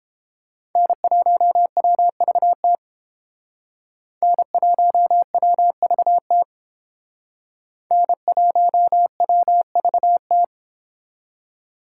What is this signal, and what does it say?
Morse code, heard as a keyed tone, about 30 wpm, 700 Hz. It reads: N1WVT N1WVT N1WVT